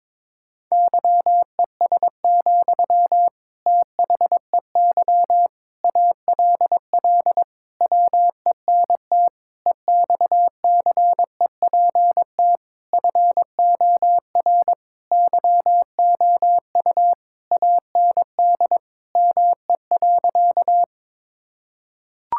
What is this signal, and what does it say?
Morse code, heard as a keyed tone, about 22 wpm, 700 Hz. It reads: YES, THEY ALL WENT EXCEPT FOR YOU AND ME.
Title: Sentences from Top 1600 Words 22wpm